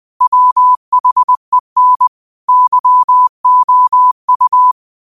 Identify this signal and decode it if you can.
Morse code, heard as a keyed tone, about 20 wpm, 1000 Hz. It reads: WHEN YOU